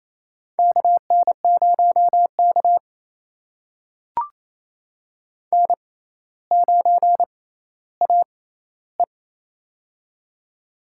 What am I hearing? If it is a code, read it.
Morse code, heard as a keyed tone, about 28 wpm, spaced as for 14 wpm, 700 Hz. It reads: KN0K N9AE
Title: USA Call Signs - 4 Characters - 6x Character Spacing - 28wpm